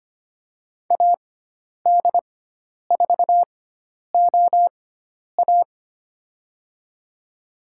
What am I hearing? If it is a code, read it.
Morse code, heard as a keyed tone, about 25 wpm, spaced as for 10 wpm, 700 Hz. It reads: AD4OA
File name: Lesson 16g - Letter c - Call Signs 25wpm